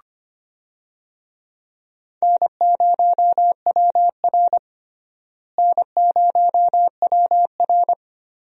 Morse code, heard as a keyed tone, about 25 wpm, 700 Hz. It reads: N0WR N0WR
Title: USA Call Signs - 4 Characters - Repeated 2x - 25wpm